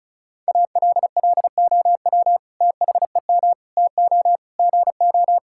Morse code, heard as a keyed tone, about 36 wpm, 700 Hz. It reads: ALLOW THEM TO GO
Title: Sentences from Top 1000 Words - Review Part 5 - 35wpm